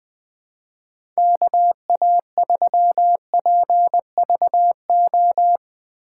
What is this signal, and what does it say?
Morse code, heard as a keyed tone, about 20 wpm, 700 Hz. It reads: KA3PVO
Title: USA Call Signs - 6 Characters - 2x Character Spacing - 20wpm